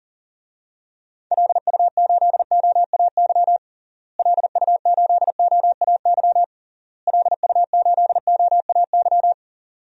Morse code, heard as a keyed tone, about 37 wpm, 700 Hz. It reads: LU8OAY LU8OAY LU8OAY
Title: DXCC-100 Call Signs - 6 Characters - Repeated 3x - 40wpm